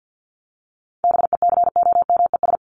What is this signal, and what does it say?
Morse code, heard as a keyed tone, about 50 wpm, 700 Hz. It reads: BECOMES
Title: ES:S Suffix - Top 100 Words with Base Words 50wpm